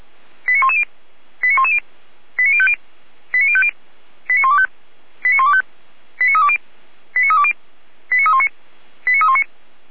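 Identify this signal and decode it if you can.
Five-tone selective call (ZVEI), heard as quick runs of five tones, tones of about 70 ms. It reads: 89110 89110 89050 89050 89125 89125 89320 89320 89319 89319
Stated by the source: ZVEI five-tone selective call received with an SDR.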